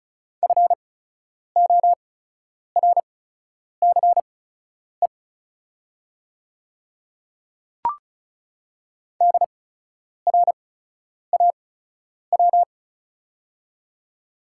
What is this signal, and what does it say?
Morse code, heard as a keyed tone, about 34 wpm, spaced as for 10 wpm, 700 Hz. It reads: FORCE DRAW
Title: 8x Character Spacing - Top 500 Words - 35wpm